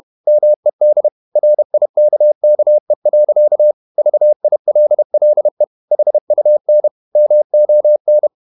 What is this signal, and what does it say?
Morse code, heard as a keyed tone, about 31 wpm, 600 Hz. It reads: MED RIKKE. VILLE HUN MON
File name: Ferie QRQ 31wpm